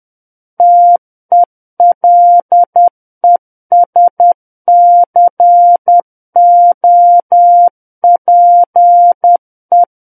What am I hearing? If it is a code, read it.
Morse code, heard as a keyed tone, about 10 wpm, 700 Hz. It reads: TELESCOPE
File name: Book One / 01-Chapter 1 - 10 WPM